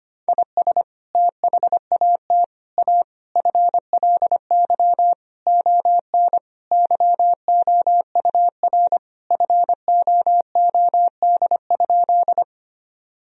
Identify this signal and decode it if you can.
Morse code, heard as a keyed tone, about 25 wpm, 700 Hz. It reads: IS THAT A FLY ON YOUR FOOD?
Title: Sentences from Top 1000 Words - Review Part 2 - 25wpm